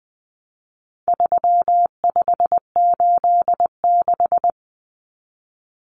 Morse code, heard as a keyed tone, about 20 wpm, 700 Hz. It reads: 3586
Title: Numbers - 4-Digit 20wpm